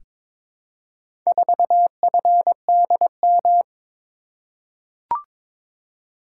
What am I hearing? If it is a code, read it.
Morse code, heard as a keyed tone, about 22 wpm, 700 Hz. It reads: 4FDM
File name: practice / 22wpm / Four Letter-Number Combinations - 22wpm